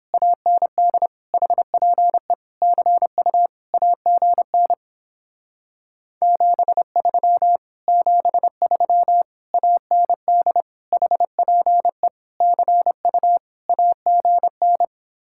Morse code, heard as a keyed tone, about 29 wpm, 700 Hz. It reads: AND HPE CU AGN 73 73 AND HPE CU AGN